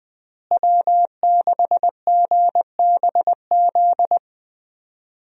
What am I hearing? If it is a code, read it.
Morse code, heard as a keyed tone, about 20 wpm, 700 Hz. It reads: W6GBZ
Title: USA Call Signs - 5 Characters - Repeated 2x - 20wpm